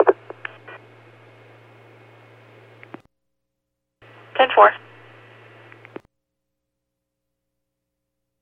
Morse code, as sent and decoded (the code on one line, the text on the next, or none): none